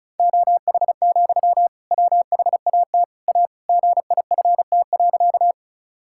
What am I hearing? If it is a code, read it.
Morse code, heard as a keyed tone, about 36 wpm, 700 Hz. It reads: OH, WHAT A GIFT.
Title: Sentences from Top 1000 Words - Review Part 6 - 35wpm